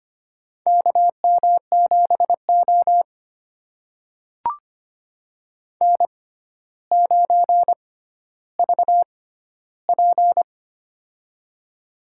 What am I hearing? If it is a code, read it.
Morse code, heard as a keyed tone, about 25 wpm, spaced as for 13 wpm, 700 Hz. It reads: KM7O N9VP
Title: USA Call Signs - 6x Character Spacing - 25wpm